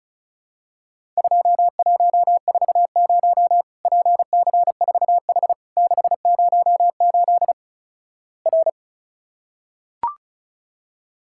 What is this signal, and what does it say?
Morse code, heard as a keyed tone, about 36 wpm, 700 Hz. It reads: 2140 PC4H 608 R
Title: Worked All Europe DX Contest - QTC Exchange - 35wpm